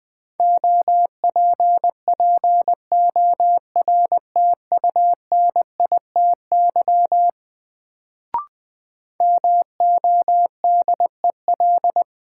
Morse code, heard as a keyed tone, about 20 wpm, 700 Hz. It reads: OPPORTUNITY MODEL
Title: Sentences from Top 1000 Words - New Word Drill - 20wpm